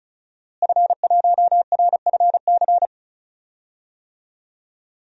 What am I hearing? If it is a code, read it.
Morse code, heard as a keyed tone, about 34 wpm, 700 Hz. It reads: F1RFC
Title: DXCC-100 Call Signs - 5 Characters - Repeated 3x - 35wpm